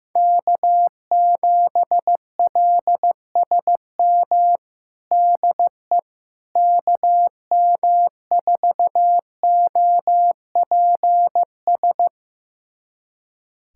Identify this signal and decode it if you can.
Morse code, heard as a keyed tone, about 15 wpm, 700 Hz. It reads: K7LSM DE KM4OPS
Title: USA Callsign DE Callsign 15wpm